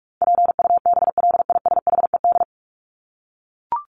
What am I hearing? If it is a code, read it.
Morse code, heard as a keyed tone, about 44 wpm, 700 Hz. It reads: PUBLISHED
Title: ED Suffix - Top 100 Words with Base Words 45wpm